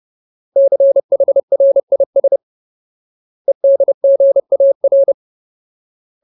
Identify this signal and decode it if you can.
Morse code, heard as a keyed tone, about 30 wpm, 550 Hz. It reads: CHRIS EDGAR
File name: Ultimate Word List / 30WPM Ultimate